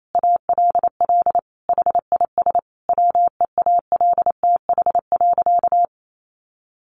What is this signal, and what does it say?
Morse code, heard as a keyed tone, about 28 wpm, 700 Hz. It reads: ALL HIS WEALTH.
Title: Sentences from Top 1200 Words 28wpm